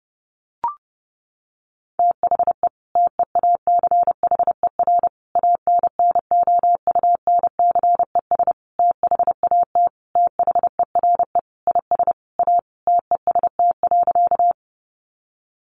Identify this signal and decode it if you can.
Morse code, heard as a keyed tone, about 30 wpm, 700 Hz. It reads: THE TEACHER ANNOUNCES THAT THERE IS A TEST.
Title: Additional Easy Phrases 30wpm